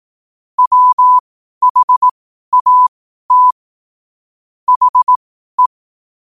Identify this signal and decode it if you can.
Morse code, heard as a keyed tone, about 18 wpm, spaced as for 12 wpm, 1000 Hz. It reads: WHAT HE
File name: qotdfiles / 12wpm